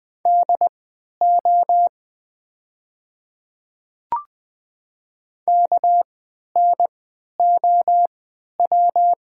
Morse code, heard as a keyed tone, about 20 wpm, spaced as for 11 wpm, 700 Hz. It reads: DO KNOW
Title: 3x Character Spacing - Top 100 Words - 20wpm